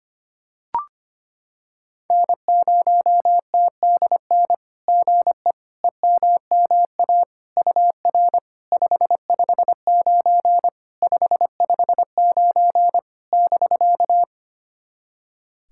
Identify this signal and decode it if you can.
Morse code, heard as a keyed tone, about 25 wpm, 700 Hz. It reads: N0TDN GE EMMA UR 559 559 <BK>
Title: SOTA QSO Practice 25wpm